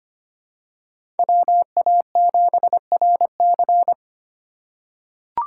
Morse code, heard as a keyed tone, about 25 wpm, 700 Hz. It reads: WA7RC
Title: USA Call Signs - 2x Character Spacing - 25wpm